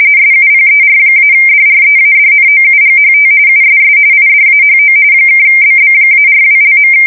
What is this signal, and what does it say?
RTTY, about 45.45 baud. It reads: SUPER-DREADNOUGHT USS IDAHO (BB-42).
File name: fsk170 / battleships